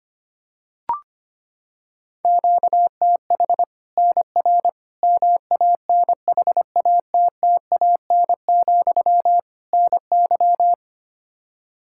Morse code, heard as a keyed tone, about 25 wpm, 700 Hz. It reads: QTH NR MANHATTAN, NY
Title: Practice QSO - Exchange 1 - Beginning - 25wpm